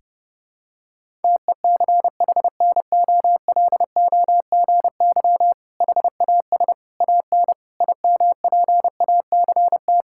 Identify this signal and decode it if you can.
Morse code, heard as a keyed tone, about 30 wpm, 700 Hz. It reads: TECHNOLOGY HAS AN IMPACT
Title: Sentences from Top 1000 Words - Review Part 6 - 30wpm